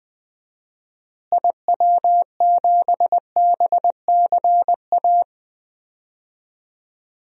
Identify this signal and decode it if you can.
Morse code, heard as a keyed tone, about 20 wpm, 700 Hz. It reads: IW7BCA
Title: DXCC-100 Call Signs - 6 Characters - Repeated 2x - 20wpm